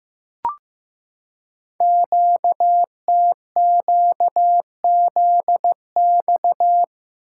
Morse code, heard as a keyed tone, about 15 wpm, 700 Hz. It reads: QTQZX
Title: Code Groups - 5 Letters - 15wpm